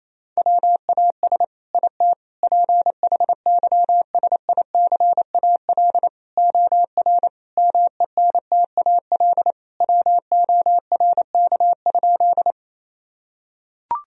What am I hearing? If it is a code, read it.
Morse code, heard as a keyed tone, about 28 wpm, 700 Hz. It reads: WAS IT PHYSICAL OR MENTAL WORK?
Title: Sentences from Top 1000 Words - Review Part 4 - 28wpm